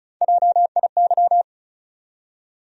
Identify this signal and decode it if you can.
Morse code, heard as a keyed tone, about 36 wpm, 700 Hz. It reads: JIY
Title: USA Call Sign Postfixes - 3 Letters 35wpm